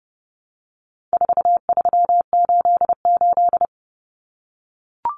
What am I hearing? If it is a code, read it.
Morse code, heard as a keyed tone, about 30 wpm, 700 Hz. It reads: 4388
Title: Common Cut Numbers - 4-Digit 30wpm